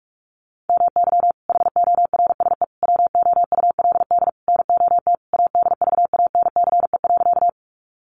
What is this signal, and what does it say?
Morse code, heard as a keyed tone, about 44 wpm, 700 Hz. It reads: MY HORSE WOULD NOT ADVANCE.
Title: Sentences from Top 1400 Words 45wpm